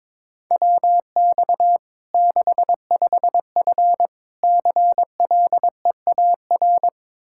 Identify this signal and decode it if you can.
Morse code, heard as a keyed tone, about 22 wpm, 700 Hz. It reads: WX 65F CLEAR
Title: Practice QSO - Exchange 2 - Info - 22wpm